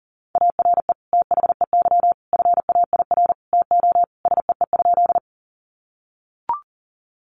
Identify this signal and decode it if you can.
Morse code, heard as a keyed tone, about 40 wpm, 700 Hz. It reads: ARE THEY FAIR TO SEE?
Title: Sentences from Top 1300 Words 40wpm